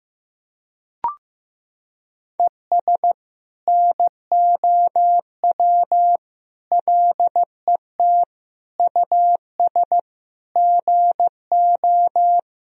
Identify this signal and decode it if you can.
Morse code, heard as a keyed tone, about 16 wpm, 700 Hz. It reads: ES NOW LET US GO
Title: Sentences from Top 300 Words and Abbreviations 15wpm